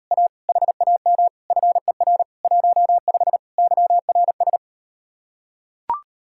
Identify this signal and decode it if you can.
Morse code, heard as a keyed tone, about 38 wpm, 700 Hz. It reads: A HAM FER 15 YRS